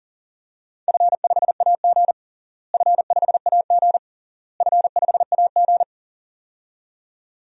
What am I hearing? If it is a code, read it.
Morse code, heard as a keyed tone, about 40 wpm, 700 Hz. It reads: F5AG F5AG F5AG
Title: DXCC-100 Call Signs - 4 Characters - Repeated 3x - 40wpm